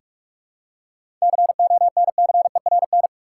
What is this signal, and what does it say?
Morse code, heard as a keyed tone, about 46 wpm, 700 Hz. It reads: CONCERN
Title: Seven Letter Words - 45wpm